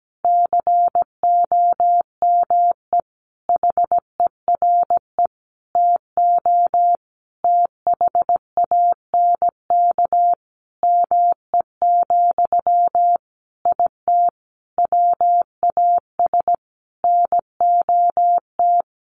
Morse code, heard as a keyed tone, about 17 wpm, 700 Hz. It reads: COME HERE TO THANK ME, IT WAS NOT
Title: Sentences from Top 800 Words 17wpm